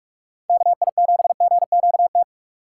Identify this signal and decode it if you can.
Morse code, heard as a keyed tone, about 45 wpm, 700 Hz. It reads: KI7GQT
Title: USA Call Signs - 6 Characters - 4x Character Spacing - 45wpm